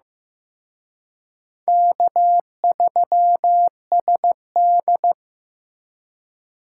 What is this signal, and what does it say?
Morse code, heard as a keyed tone, about 18 wpm, 700 Hz. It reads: K3SD
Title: Speed Racing - USA 1x2 Call Signs 15wpm